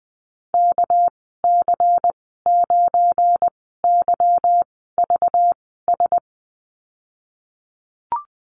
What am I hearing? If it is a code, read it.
Morse code, heard as a keyed tone, about 20 wpm, spaced as for 14 wpm, 700 Hz. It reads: KC9YVS
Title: 238 Lesson 38g - Letter q - Call Signs 20wpm